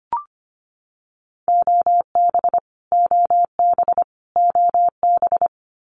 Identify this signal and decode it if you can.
Morse code, heard as a keyed tone, about 25 wpm, 700 Hz. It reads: OB OB OB
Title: Abbreviations Repeated 3x - 25wpm